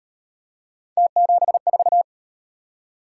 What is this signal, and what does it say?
Morse code, heard as a keyed tone, about 35 wpm, 700 Hz. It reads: T74